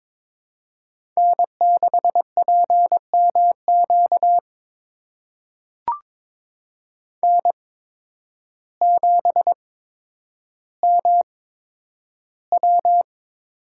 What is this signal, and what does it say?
Morse code, heard as a keyed tone, about 22 wpm, spaced as for 10 wpm, 700 Hz. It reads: N6PMQ N7MW
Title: USA Call Signs - 5 Characters - 8x Character Spacing - 22wpm